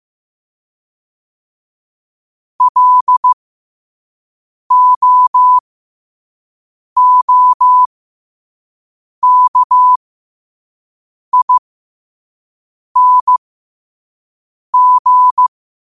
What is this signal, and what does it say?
Morse code, heard as a keyed tone, about 15 wpm, spaced as for 5 wpm, 1000 Hz. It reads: LOOKING
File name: qotdfiles / 5wpm